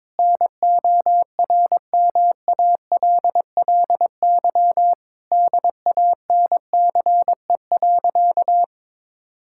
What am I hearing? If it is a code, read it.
Morse code, heard as a keyed tone, about 22 wpm, 700 Hz. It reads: NORMALLY DANCE.
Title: Sentences from Top 1000 Words - Review Part 6 - 22wpm